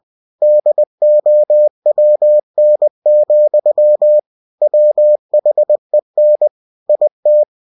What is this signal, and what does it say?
Morse code, heard as a keyed tone, about 20 wpm, 600 Hz. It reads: DOWN, WHEN IT
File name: winnie-20wpm